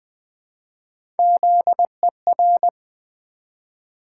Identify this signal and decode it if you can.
Morse code, heard as a keyed tone, about 20 wpm, 700 Hz. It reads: ZER
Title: Build A Word - 1 Letter At A Time - 3 to 5-Letter Words - 20wpm